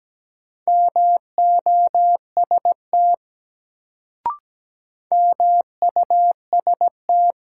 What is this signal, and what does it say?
Morse code, heard as a keyed tone, about 17 wpm, 700 Hz. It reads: MOST MUST
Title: Sentences from Top 200 Words - New Word Drill - 17wpm